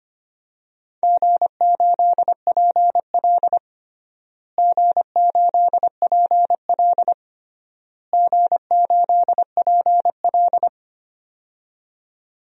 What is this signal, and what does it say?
Morse code, heard as a keyed tone, about 25 wpm, 700 Hz. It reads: G8PL G8PL G8PL
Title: DXCC-100 Call Signs - 4 Characters - Repeated 3x - 25wpm